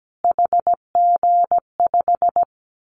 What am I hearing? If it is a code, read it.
Morse code, heard as a keyed tone, about 17 wpm, 700 Hz. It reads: HG5
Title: Callsign Prefixes for Top 25 DXCC - Repeated 3x - 17wpm